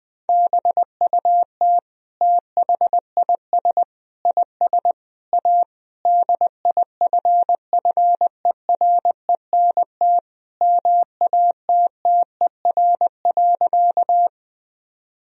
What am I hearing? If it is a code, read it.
Morse code, heard as a keyed tone, about 20 wpm, 700 Hz. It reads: BUT THIS IS A DIFFERENT MATTER.
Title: Sentences from Top 1000 Words - Review Part 3 - 20wpm